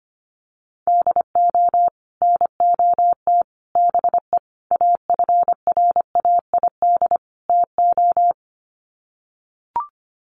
Morse code, heard as a keyed tone, about 25 wpm, 700 Hz. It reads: DO NOT BE AFRAID TO
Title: Sets of 5 Words Encore2 25wpm